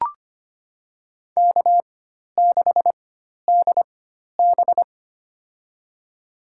Morse code, heard as a keyed tone, about 25 wpm, spaced as for 12 wpm, 700 Hz. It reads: K6DB
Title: USA Call Signs - 4 Characters - 4x Character Spacing - 25wpm